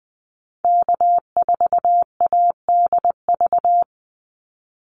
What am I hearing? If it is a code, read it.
Morse code, heard as a keyed tone, about 20 wpm, 700 Hz. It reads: K4ADV
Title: NA SOTA Activator Callsigns 20wpm